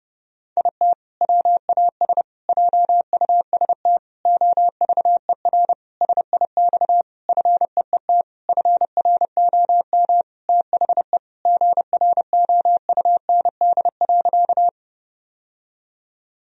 Morse code, heard as a keyed tone, about 30 wpm, 700 Hz. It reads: IT WAS JUST OVER SIX FEET FROM THE GROUND.
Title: Sentences from Top 1000 Words - Review Part 2 - 30wpm